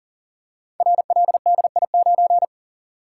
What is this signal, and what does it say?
Morse code, heard as a keyed tone, about 41 wpm, 700 Hz. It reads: RLDI9
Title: Code Groups - 5 Letter:Numbers - 40wpm